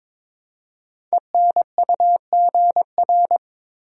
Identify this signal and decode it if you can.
Morse code, heard as a keyed tone, about 22 wpm, 700 Hz. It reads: ENUGR
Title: Code Groups - 5 Letters - 22wpm